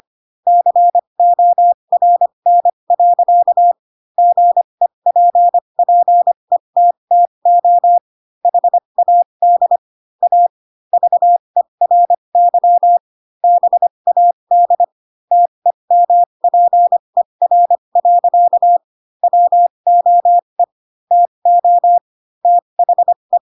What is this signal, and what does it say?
Morse code, heard as a keyed tone, about 25 wpm, spaced as for 22 wpm, 700 Hz. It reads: CORN. GEPPETTO HAD A VERY BAD TEMPER. WOE TO THE
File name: Pinocchio / Pinocchio ch2 18 25 ik1hge